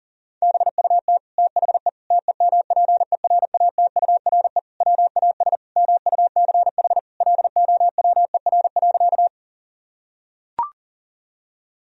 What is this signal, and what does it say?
Morse code, heard as a keyed tone, about 40 wpm, 700 Hz. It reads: BUT THE TEMPERATURE WAS MUCH LOWER.
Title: Sentences from Top 1000 Words - Review Part 4 - 40wpm